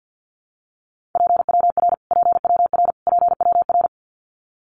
Morse code, heard as a keyed tone, about 50 wpm, 700 Hz. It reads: PWR PWR PWR
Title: Abbreviations Repeated 3x - 50wpm